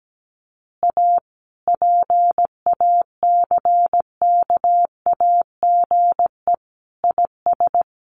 Morse code, heard as a keyed tone, about 17 wpm, 700 Hz. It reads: A PACKAGE IS
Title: Additional Easy Phrases 17wpm